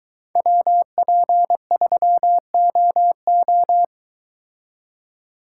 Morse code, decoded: WP3OO

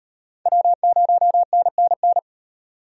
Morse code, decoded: W0NNN

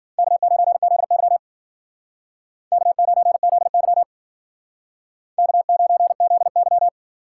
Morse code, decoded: K9ZY K9ZY K9ZY

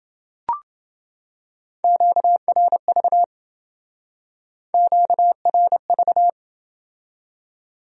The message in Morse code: QRV QRV

